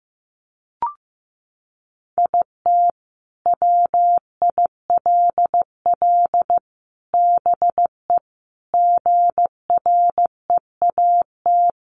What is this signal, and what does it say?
Morse code, heard as a keyed tone, about 15 wpm, 700 Hz. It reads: IT WILL BE GREAT
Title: Sentences from Top 700 Words 15wpm